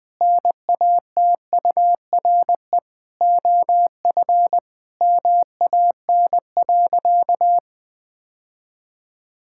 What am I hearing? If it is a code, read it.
Morse code, heard as a keyed tone, about 20 wpm, 700 Hz. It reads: NATURE OF MAN.
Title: Sentences from Top 1000 Words - Review Part 6 - 20wpm